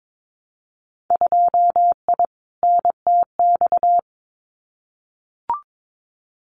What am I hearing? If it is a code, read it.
Morse code, heard as a keyed tone, about 22 wpm, 700 Hz. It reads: 2I NTX
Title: Winter Field Day Exchanges Encore 22wpm